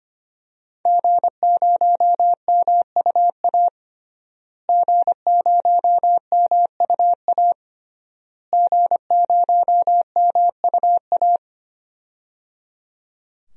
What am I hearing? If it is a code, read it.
Morse code, heard as a keyed tone, about 25 wpm, 700 Hz. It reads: G0MUA G0MUA G0MUA
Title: DXCC-100 Call Signs - 5 Characters - Repeated 3x - 25wpm